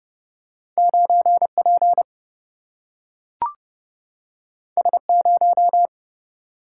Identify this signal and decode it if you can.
Morse code, heard as a keyed tone, about 31 wpm, 700 Hz. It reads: 9 P S 0
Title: Single-Letter Single-Number 30wpm